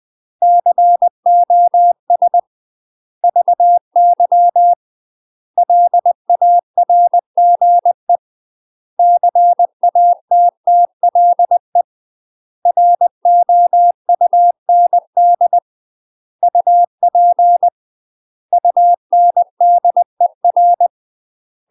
Morse code, heard as a keyed tone, about 20 wpm, 700 Hz. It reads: COS VY LARGE CATTLE ROUND UP UNDER
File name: UK MASTER QSOs TRAINING / 20WPM / UK MASTER QSOs 0032